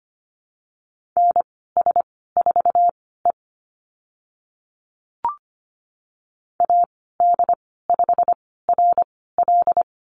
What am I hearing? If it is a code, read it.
Morse code, heard as a keyed tone, about 25 wpm, spaced as for 16 wpm, 700 Hz. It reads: NS4E AD5RL
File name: Lesson 15g - Number 5 - Call Signs 25wpm